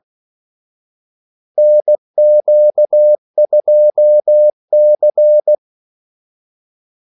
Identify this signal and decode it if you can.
Morse code, heard as a keyed tone, about 16 wpm, 600 Hz. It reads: NQ2C
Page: Kaldesignaler 16 wpm